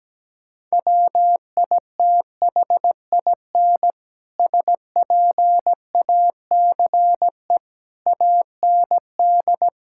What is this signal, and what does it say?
Morse code, heard as a keyed tone, about 17 wpm, 700 Hz. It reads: WITHIN SPACE AND
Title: Sentences from Top 1000 Words - Review Part 5 - 17wpm